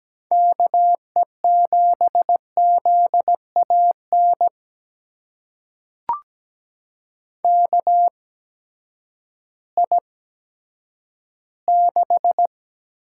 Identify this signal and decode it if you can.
Morse code, heard as a keyed tone, about 17 wpm, spaced as for 10 wpm, 700 Hz. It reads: KE7ZAN KI6
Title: USA Call Signs - 6 Characters - 8x Character Spacing - 17wpm